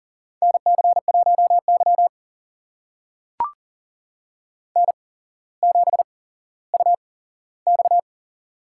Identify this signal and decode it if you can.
Morse code, heard as a keyed tone, about 44 wpm, spaced as for 18 wpm, 700 Hz. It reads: NC1Y N7UX
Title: USA Call Signs - 4 Characters - 8x Character Spacing - 40wpm